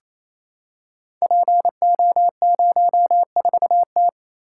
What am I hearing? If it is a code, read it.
Morse code, heard as a keyed tone, about 28 wpm, 700 Hz. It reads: PO04T
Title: Code Groups - 5 Letter:Numbers - 28wpm